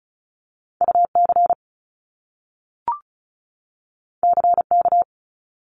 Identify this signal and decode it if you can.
Morse code, heard as a keyed tone, about 36 wpm, 700 Hz. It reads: UC CK
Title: Top Two Letter Combinations in Words 35wpm